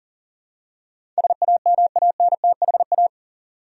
Morse code, heard as a keyed tone, about 40 wpm, 700 Hz. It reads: SAMANTHA
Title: USA Common Names - 40wpm